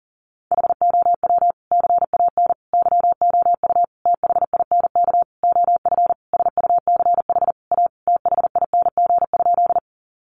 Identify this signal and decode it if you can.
Morse code, heard as a keyed tone, about 40 wpm, 700 Hz. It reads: HOW CAN YOU THINK OF SUCH A THING?
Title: Sentences from Top 200 Words 40wpm